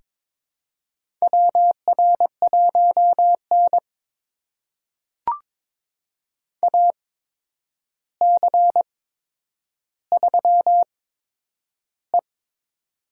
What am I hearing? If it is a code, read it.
Morse code, heard as a keyed tone, about 22 wpm, spaced as for 9 wpm, 700 Hz. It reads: WR1N AC3E
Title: USA Call Signs - 4 Characters - 8x Character Spacing - 22wpm